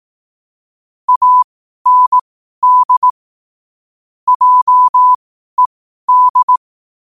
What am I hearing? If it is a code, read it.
Morse code, heard as a keyed tone, about 18 wpm, spaced as for 12 wpm, 1000 Hz. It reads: AND JED
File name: qotdfiles / 12wpm